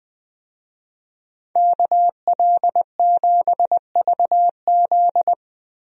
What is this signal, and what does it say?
Morse code, heard as a keyed tone, about 20 wpm, 700 Hz. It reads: KL7VZ
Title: DXCC-100 Call Signs - 5 Characters - 20wpm